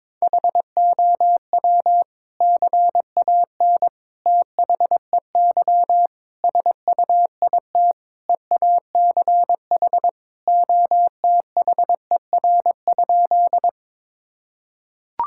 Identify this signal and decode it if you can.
Morse code, heard as a keyed tone, about 22 wpm, 700 Hz. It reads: HOW CAN THEY SUIT EACH OTHER?
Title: Sentences from Top 1600 Words 22wpm